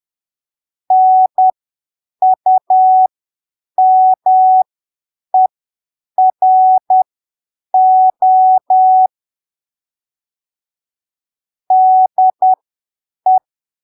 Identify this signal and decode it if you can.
Morse code, heard as a keyed tone, about 10 wpm, spaced as for 7 wpm, 750 Hz. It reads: NUMERO DE
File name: arquivos / texto / texto 22 10ppm